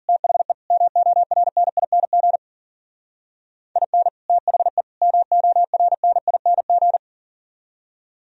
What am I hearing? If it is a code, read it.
Morse code, heard as a keyed tone, about 50 wpm, 700 Hz. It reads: THE MORNING IN THE MORNING